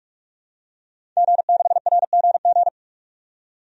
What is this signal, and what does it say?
Morse code, heard as a keyed tone, about 43 wpm, 700 Hz. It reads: G6RGG
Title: DXCC-100 Call Signs - 5 Characters - 45wpm